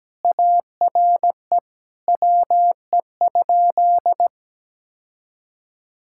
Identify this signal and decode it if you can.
Morse code, heard as a keyed tone, about 17 wpm, 700 Hz. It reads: ARE WE?
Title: Sentences from Top 1600 Words 17wpm